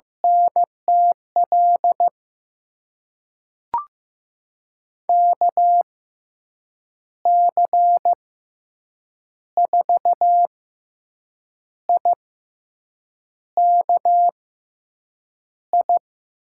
Morse code, heard as a keyed tone, about 16 wpm, spaced as for 6 wpm, 700 Hz. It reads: NTL KC4IKI
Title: USA Call Signs - 6 Characters - 6x Character Spacing - 15wpm